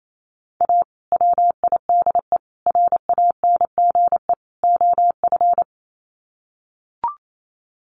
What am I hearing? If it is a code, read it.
Morse code, heard as a keyed tone, about 28 wpm, 700 Hz. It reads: A WIDE RANGE OF
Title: Sets of 4 Words 28wpm